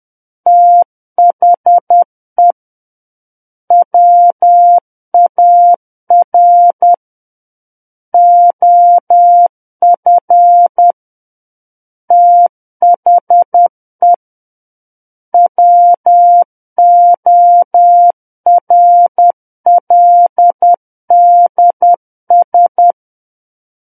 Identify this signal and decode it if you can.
Morse code, heard as a keyed tone, about 10 wpm, 700 Hz. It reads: THE WAR OF THE WORLDS
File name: Book One / 01-Chapter 1 - 10 WPM